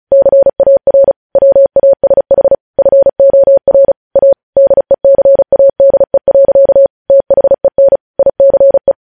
Morse code, heard as a keyed tone, about 36 wpm, 550 Hz. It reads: CAR WASH FOR A DECADE. THEN ICE
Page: news headlines in morse code at 35 wpm